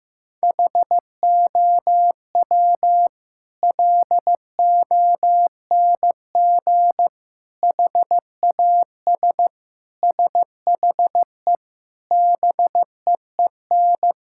Morse code, heard as a keyed tone, about 15 wpm, 700 Hz. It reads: HOW LONG HAS SHE BEEN
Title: Sentences from Top 1000 Words - Review Part 6 - 15wpm